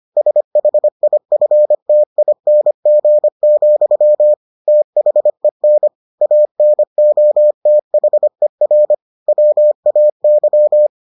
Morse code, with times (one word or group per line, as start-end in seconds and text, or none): 0.2-4.3 SHIFTING,
4.7-5.9 THEN
6.2-8.9 ANOTHER
9.3-10.9 WAY